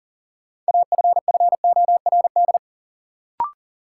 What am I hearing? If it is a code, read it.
Morse code, heard as a keyed tone, about 40 wpm, 700 Hz. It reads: AFFORD
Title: Sentences from Top 1100 Words - New Word Drill - 40wpm